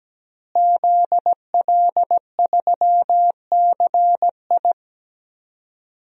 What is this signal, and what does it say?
Morse code, heard as a keyed tone, about 17 wpm, 700 Hz. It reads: ZL3CI
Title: DXCC-100 Call Signs - 5 Characters - Repeated 2x - 17wpm